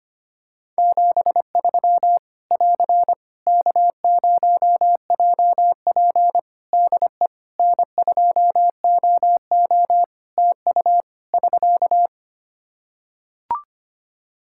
Morse code, heard as a keyed tone, about 25 wpm, 700 Hz. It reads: 73 <AR> K0JP DE N2OO TU <SK>
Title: Practice QSO - Exchange 4 - Ending - 25wpm